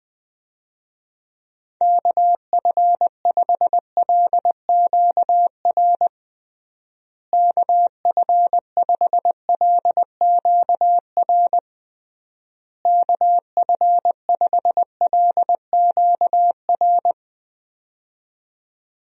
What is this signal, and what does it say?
Morse code, heard as a keyed tone, about 21 wpm, 700 Hz. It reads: KF5LQR KF5LQR KF5LQR
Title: USA Call Signs - 6 Characters - Repeated 3x - 20wpm